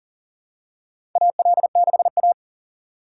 Morse code, heard as a keyed tone, about 37 wpm, 700 Hz. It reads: AL6A
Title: USA Call Signs - 4 Characters - 6x Character Spacing - 40wpm